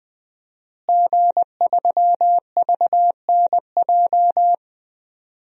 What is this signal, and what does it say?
Morse code, heard as a keyed tone, about 21 wpm, 700 Hz. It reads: G3VNJ
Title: DXCC-100 Call Signs - 5 Characters - Repeated 3x - 20wpm